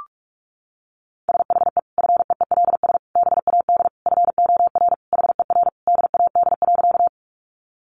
Morse code, heard as a keyed tone, about 44 wpm, 700 Hz. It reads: SHE FEELS BAD FOR HER DAD.
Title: Additional Easy Phrases 45wpm